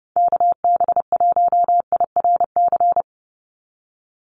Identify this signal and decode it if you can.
Morse code, heard as a keyed tone, about 30 wpm, 700 Hz. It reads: KB1IRC
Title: USA Call Signs 30wpm